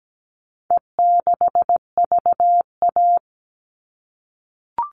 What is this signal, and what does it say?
Morse code, heard as a keyed tone, about 17 wpm, 700 Hz. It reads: E6VA
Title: All Cut Numbers - 4-Digit 17wpm